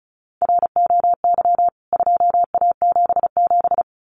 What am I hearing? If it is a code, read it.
Morse code, heard as a keyed tone, about 34 wpm, 700 Hz. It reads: ROY 2A77
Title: CWT Exchange 35wpm